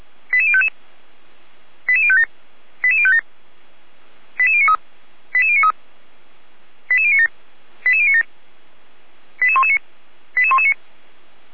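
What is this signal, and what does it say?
Five-tone selective call (ZVEI), heard as quick runs of five tones, tones of about 70 ms. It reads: 88055 88057 88057 88093 88093 88097 88097 88119 88119
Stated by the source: ZVEI five-tone call received by SDR, from feuerwehr-orlamuende.org audio archive.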